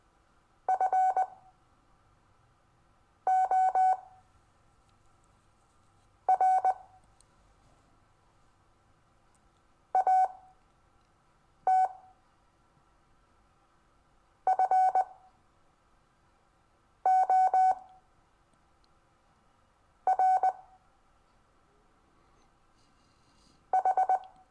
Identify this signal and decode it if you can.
Morse code, heard as a keyed tone, about 20 wpm, spaced as for 4 wpm, 750 Hz. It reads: FOR AT FOR H
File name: cw / Lesson22 REVIEW